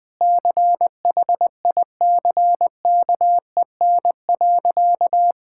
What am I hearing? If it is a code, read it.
Morse code, heard as a keyed tone, about 20 wpm, 700 Hz. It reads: CHICKEN.
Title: Sentences from Top 1000 Words - Review Part 5 - 20wpm